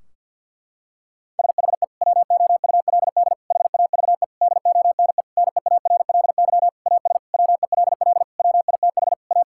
Morse code, heard as a keyed tone, about 49 wpm, 700 Hz. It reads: SHE WOULD HAVE DONE NEARLY AS WELL WITH A
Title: Sentences from Top 1600 Words 50wpm